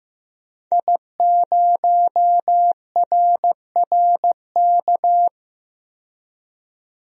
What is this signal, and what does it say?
Morse code, heard as a keyed tone, about 15 wpm, 700 Hz. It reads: I0RRK
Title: DXCC-100 Call Signs - 5 Characters - 15wpm